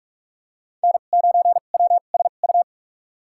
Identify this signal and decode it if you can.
Morse code, heard as a keyed tone, about 46 wpm, spaced as for 32 wpm, 700 Hz. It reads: N9WSU
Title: USA Call Signs - 5 Characters - 2x Character Spacing - 45wpm